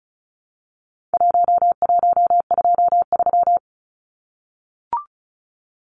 Morse code, heard as a keyed tone, about 34 wpm, 700 Hz. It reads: 1123
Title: Numbers - 4-Digit 35wpm